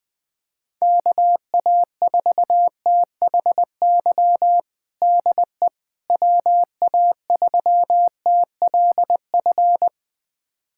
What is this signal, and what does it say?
Morse code, heard as a keyed tone, about 20 wpm, 700 Hz. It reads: KA4THY DE WA3TLF
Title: USA Callsign DE Callsign 20wpm